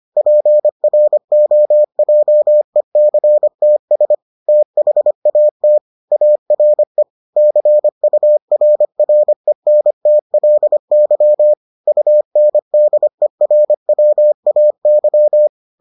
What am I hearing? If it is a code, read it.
Morse code, heard as a keyed tone, about 25 wpm, 600 Hz. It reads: PROJECTS THAT ARE CURRENTLY UNDERWAY